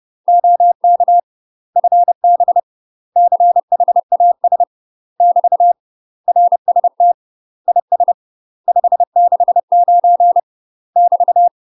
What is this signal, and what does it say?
Morse code, heard as a keyed tone, about 30 wpm, 700 Hz. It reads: OK FB CHAS <BT> RST IS 569 <BT>
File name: UK MASTER QSOs TRAINING / 30WPM / UK MASTER QSOs 0034